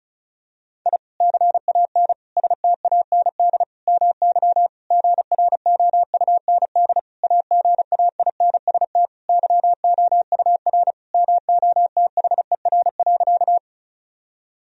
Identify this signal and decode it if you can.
Morse code, heard as a keyed tone, about 36 wpm, 700 Hz. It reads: I CAN STAND MY GROUND AGAINST YOUR MOTHER.
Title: Sentences from Top 1000 Words - Review Part 2 - 35wpm